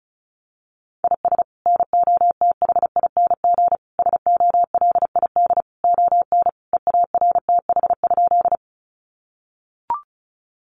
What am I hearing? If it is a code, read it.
Morse code, heard as a keyed tone, about 34 wpm, 700 Hz. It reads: IS NOTHING SOLID ON EARTH?
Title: Sentences from Top 1500 Words 35wpm